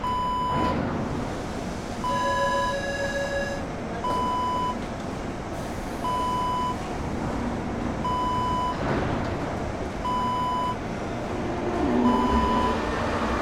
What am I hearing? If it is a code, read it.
There is no code.